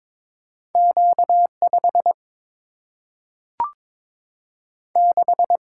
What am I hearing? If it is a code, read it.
Morse code, heard as a keyed tone, about 22 wpm, 700 Hz. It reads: Q 5 6
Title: Single-Letter Single-Number 22wpm